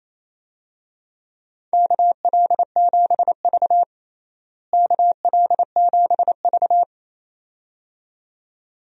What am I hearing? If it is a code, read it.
Morse code, heard as a keyed tone, about 28 wpm, 700 Hz. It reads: KL7V KL7V
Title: DXCC-100 Call Signs - 4 Characters - Repeated 2x - 28wpm